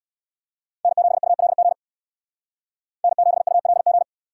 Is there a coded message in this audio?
Yes: Morse code from 0.8 to 4.0 s, heard as a keyed tone, about 77 wpm, 700 Hz. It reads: N7ULG N7ULG